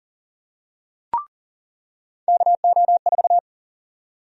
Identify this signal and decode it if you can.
Morse code, heard as a keyed tone, about 40 wpm, 700 Hz. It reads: KO4
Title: USA Call Sign Prefixes - 40wpm